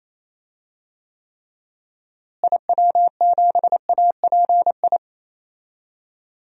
Morse code, heard as a keyed tone, about 28 wpm, 700 Hz. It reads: IW7API
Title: DXCC-100 Call Signs - 6 Characters - Repeated 3x - 28wpm